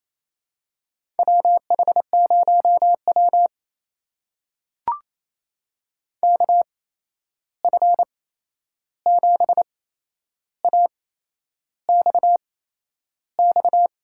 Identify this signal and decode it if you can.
Morse code, heard as a keyed tone, about 28 wpm, spaced as for 11 wpm, 700 Hz. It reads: WH0W KF7AXX
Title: USA Call Signs - 8x Character Spacing - 28wpm